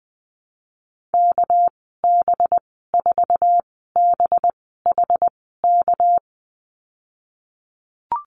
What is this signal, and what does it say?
Morse code, heard as a keyed tone, about 20 wpm, spaced as for 15 wpm, 700 Hz. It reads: KB4BHK
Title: 245 Lesson 39g - Letter z - Call Signs 20wpm